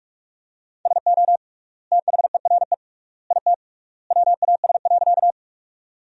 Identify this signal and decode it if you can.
Morse code, heard as a keyed tone, about 46 wpm, 700 Hz. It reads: SO THERE IT WAS.
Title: 3x Word-Spacing - Sentences from Top 100 Words - 45wpm